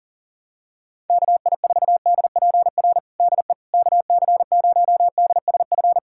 Morse code, heard as a keyed tone, about 40 wpm, 700 Hz. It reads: KI4DPR DE KC0DSF
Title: USA Callsign DE Callsign 40wpm